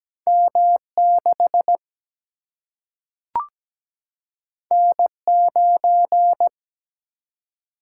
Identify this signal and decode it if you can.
Morse code, heard as a keyed tone, about 17 wpm, 700 Hz. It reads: M 6 N 9
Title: Single-Letter Single-Number 17wpm